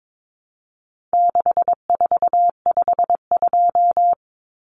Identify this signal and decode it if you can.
Morse code, heard as a keyed tone, about 22 wpm, 700 Hz. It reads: 6452
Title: Common Cut Numbers - 4-Digit 22wpm